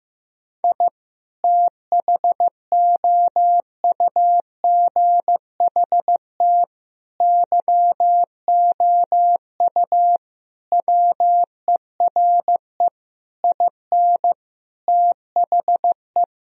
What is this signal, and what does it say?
Morse code, heard as a keyed tone, about 15 wpm, 700 Hz. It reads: I THOUGHT YOU WERE IN THE
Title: Sentences from Top 1000 Words - Review Part 1 - 15wpm